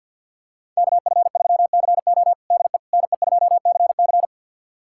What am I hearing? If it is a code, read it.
Morse code, heard as a keyed tone, about 48 wpm, 700 Hz. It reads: KF3CO DE NE2CC
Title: USA Callsign DE Callsign 50wpm